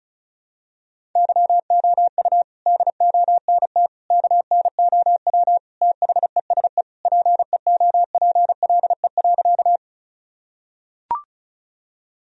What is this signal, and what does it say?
Morse code, heard as a keyed tone, about 36 wpm, 700 Hz. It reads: YOU DONT KNOW THESE PEOPLE.
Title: Sentences from Top 1000 Words - Review Part 1 - 35wpm